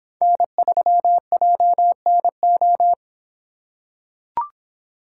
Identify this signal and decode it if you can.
Morse code, heard as a keyed tone, about 25 wpm, 700 Hz. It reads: N3JNO